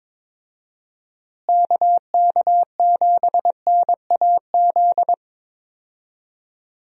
Morse code, heard as a keyed tone, about 22 wpm, 700 Hz. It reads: KK7NAZ
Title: USA Call Signs - 6 Characters - Repeated 3x - 22wpm